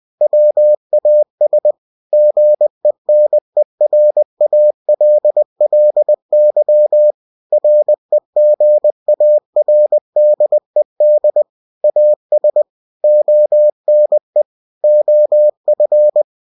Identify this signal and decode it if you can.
Morse code, heard as a keyed tone, about 20 wpm, 600 Hz. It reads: WAS GENERALLY REGARDED AS ONE OF
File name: cw-books / winnie-the-pooh / winnie-20wpm-page-0062